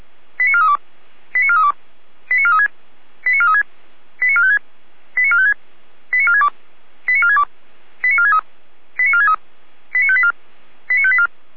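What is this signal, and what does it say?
Five-tone selective call (ZVEI), heard as quick runs of five tones, tones of about 70 ms. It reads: 89432 89432 89436 89436 89456 89456 89462 89462 89463 89463 89584 89584
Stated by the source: ZVEI five-tone selective call received on SDR.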